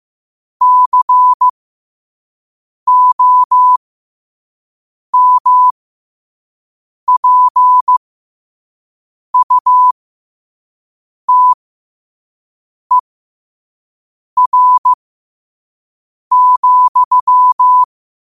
Morse code, heard as a keyed tone, about 15 wpm, spaced as for 5 wpm, 1000 Hz. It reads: COMPUTER,
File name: qotdfiles / 5wpm